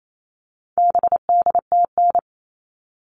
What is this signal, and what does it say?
Morse code, heard as a keyed tone, about 28 wpm, 700 Hz. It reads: BDTN
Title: All Cut Numbers - 4-Digit 28wpm